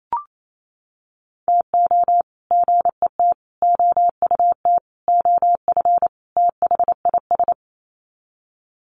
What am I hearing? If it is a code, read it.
Morse code, heard as a keyed tone, about 28 wpm, 700 Hz. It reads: TO GET OUT OF THIS
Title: Sets of 5 Words Encore2 28wpm